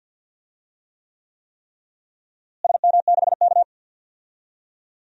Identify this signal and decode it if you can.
Morse code, heard as a keyed tone, about 51 wpm, 700 Hz. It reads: SM6K
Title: DXCC-100 Call Signs - 4 Characters - Repeated 3x - 50wpm